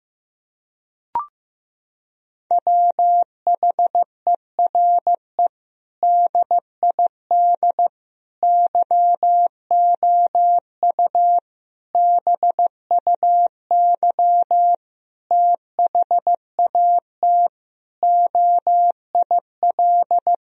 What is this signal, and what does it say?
Morse code, heard as a keyed tone, about 16 wpm, 700 Hz. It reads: WHERE DID YOU BUY THAT OIL
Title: Sentences from Top 1000 Words - Review Part 4 - 15wpm